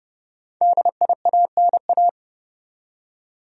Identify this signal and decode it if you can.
Morse code, heard as a keyed tone, about 31 wpm, 700 Hz. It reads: DIANA
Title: USA Common Names - Max 5 Letters - 30wpm